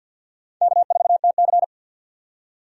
Morse code, heard as a keyed tone, about 52 wpm, 700 Hz. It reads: K4TC
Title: USA Call Signs - 4 Characters - 3x Character Spacing - 50wpm